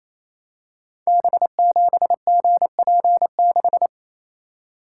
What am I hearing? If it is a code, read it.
Morse code, heard as a keyed tone, about 27 wpm, 700 Hz. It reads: B7GP6
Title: Code Groups - 5 Letter:Numbers - 28wpm